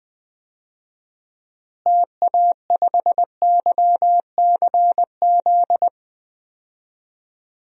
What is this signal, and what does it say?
Morse code, heard as a keyed tone, about 20 wpm, 700 Hz. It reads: TA5YCZ
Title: DXCC-100 Call Signs - 6 Characters - Repeated 3x - 20wpm